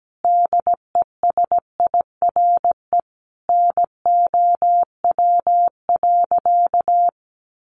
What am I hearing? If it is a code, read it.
Morse code, heard as a keyed tone, about 17 wpm, 700 Hz. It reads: DESIRE NOW.
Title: Sentences from Top 1100 Words 17wpm